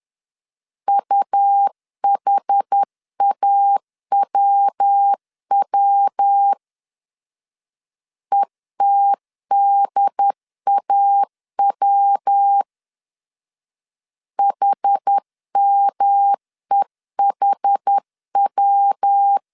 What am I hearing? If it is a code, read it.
Morse code, heard as a keyed tone, about 10 wpm, 800 Hz. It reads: UHAWW ETDAW HMEHW
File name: 10wpm